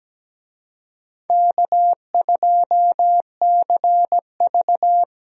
Morse code, heard as a keyed tone, about 17 wpm, 700 Hz. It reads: K2CV
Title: USA Call Signs - 4 Characters - Repeated 3x - 17wpm